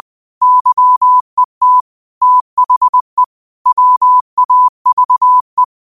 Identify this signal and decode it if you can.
Morse code, heard as a keyed tone, about 20 wpm, 1000 Hz. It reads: YET THE WAVE